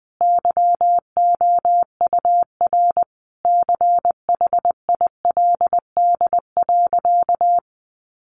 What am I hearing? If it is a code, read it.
Morse code, heard as a keyed tone, about 20 wpm, 700 Hz. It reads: YOUR CHILD.
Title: Sentences from Top 800 Words 20wpm